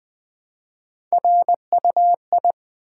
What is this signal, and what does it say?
Morse code, heard as a keyed tone, about 20 wpm, 700 Hz. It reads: RUI
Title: Build A Word - 1 Letter At A Time - 4-Letter Words - 20wpm